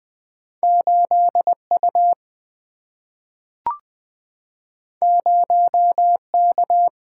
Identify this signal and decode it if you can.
Morse code, heard as a keyed tone, about 20 wpm, 700 Hz. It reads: 8 U 0 K
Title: Single-Letter Single-Number 20wpm